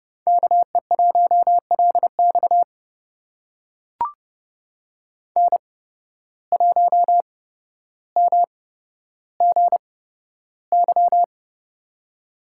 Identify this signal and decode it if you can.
Morse code, heard as a keyed tone, about 30 wpm, spaced as for 13 wpm, 700 Hz. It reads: KE1LX N1MGY
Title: USA Call Signs - 5 Characters - 8x Character Spacing - 30wpm